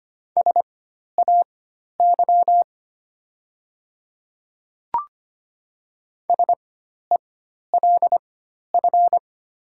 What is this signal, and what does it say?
Morse code, heard as a keyed tone, about 25 wpm, spaced as for 12 wpm, 700 Hz. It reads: SAY SELF
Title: 4x Character Spacing - Top 500 Words - 25wpm